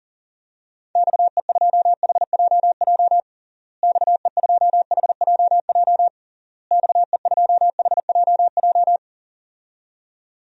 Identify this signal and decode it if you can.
Morse code, heard as a keyed tone, about 40 wpm, 700 Hz. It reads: XE2HJJ XE2HJJ XE2HJJ
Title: DXCC-100 Call Signs - 6 Characters - Repeated 3x - 40wpm